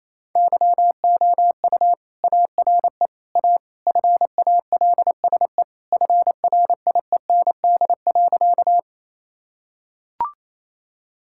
Morse code, heard as a keyed tone, about 27 wpm, 700 Hz. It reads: YOU ARE A FALSE FRIEND.
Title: Sentences from Top 1600 Words 28wpm